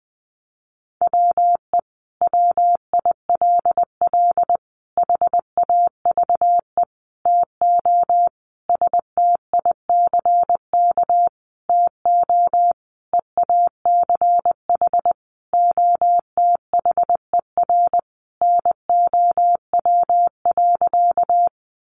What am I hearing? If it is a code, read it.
Morse code, heard as a keyed tone, about 20 wpm, 700 Hz. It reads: WE WILL HAVE TO STICK TO EACH OTHER NOW.
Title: Sentences from Top 1200 Words 20wpm